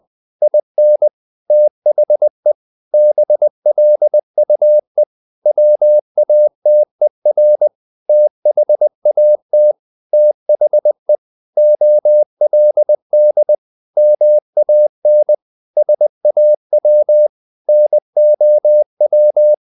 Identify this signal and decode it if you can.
Morse code, heard as a keyed tone, about 20 wpm, 600 Hz. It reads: IN THE BLUE WATER THAT THE OLD MAN SAW NOW
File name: the old man and the sea / the old man and the sea part 1 20wpm